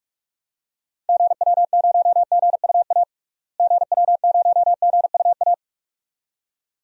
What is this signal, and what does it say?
Morse code, heard as a keyed tone, about 44 wpm, 700 Hz. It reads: GW0GUA GW0GUA
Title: DXCC-100 Call Signs - 6 Characters - Repeated 2x - 45wpm